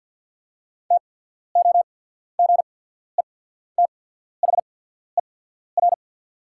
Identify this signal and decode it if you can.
Morse code, heard as a keyed tone, about 48 wpm, spaced as for 14 wpm, 700 Hz. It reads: TOGETHER
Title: 8x Character Spacing - Top 300 Words - 50wpm